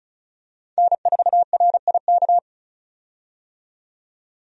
Morse code, heard as a keyed tone, about 36 wpm, 700 Hz. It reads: N4RIK
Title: USA Call Signs - 5 Characters - 35wpm